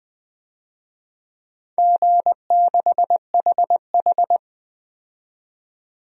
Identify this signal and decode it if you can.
Morse code, heard as a keyed tone, about 20 wpm, 700 Hz. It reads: G6HH
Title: DXCC-100 Call Signs - 4 Characters - 20wpm